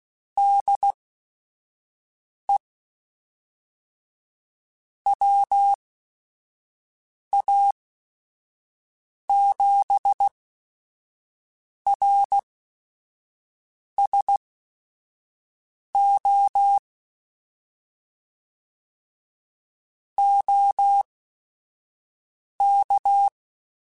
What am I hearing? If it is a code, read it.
Morse code, heard as a keyed tone, about 16 wpm, spaced as for 5 wpm, 800 Hz. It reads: DE WA7RSO OK
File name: QSOs / QSOs-mp3s / WD0FBT 1@5